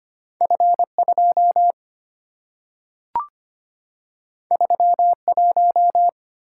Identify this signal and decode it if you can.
Morse code, heard as a keyed tone, about 25 wpm, 700 Hz. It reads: F 2 3 1
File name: Two Letter-Number Combinations - 25wpm